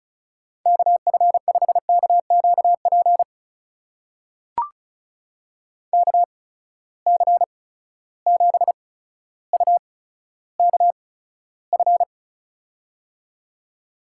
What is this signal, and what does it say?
Morse code, heard as a keyed tone, about 36 wpm, spaced as for 15 wpm, 700 Hz. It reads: KF5KQP KC7UKF
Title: USA Call Signs - 6 Characters - 8x Character Spacing - 35wpm